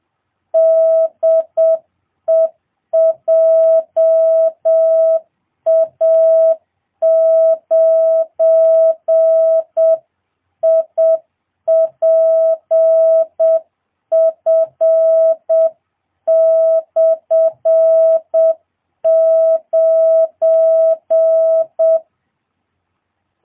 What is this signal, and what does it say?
Morse code, heard as a keyed tone, about 7 wpm, 650 Hz. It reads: DE JA9IPF/9